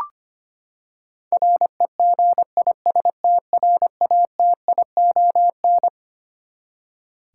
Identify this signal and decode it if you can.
Morse code, heard as a keyed tone, about 25 wpm, 700 Hz. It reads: REGISTRATION
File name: Twelve Letter Words - 25wpm